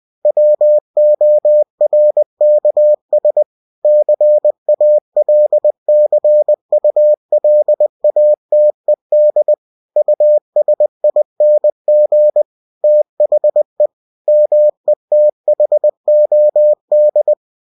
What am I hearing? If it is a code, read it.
Morse code, heard as a keyed tone, about 20 wpm, 600 Hz. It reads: WORKS CALCULATED USING THE METHOD
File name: cw-books / winnie-the-pooh / winnie-20wpm-page-0117